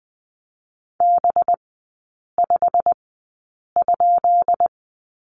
Morse code, heard as a keyed tone, about 20 wpm, spaced as for 8 wpm, 700 Hz. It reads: B5?
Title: 218 Lesson 35e - Number 0 - Three Characters 20wpm